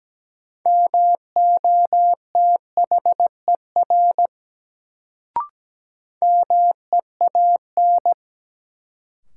Morse code, heard as a keyed tone, about 17 wpm, 700 Hz. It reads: MOTHER MEAN
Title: Sentences from Top 200 Words - New Word Drill - 17wpm